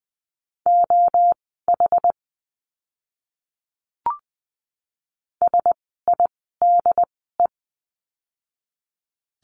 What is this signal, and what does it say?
Morse code, heard as a keyed tone, about 20 wpm, spaced as for 14 wpm, 700 Hz. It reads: OH SIDE
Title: 068 Lesson 12f - Letter d - Words 20wpm